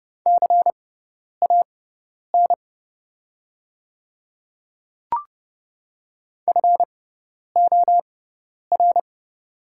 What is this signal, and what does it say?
Morse code, heard as a keyed tone, about 30 wpm, spaced as for 10 wpm, 700 Hz. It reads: CAN FOR
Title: 6x Character Spacing - Top 100 Words - 30wpm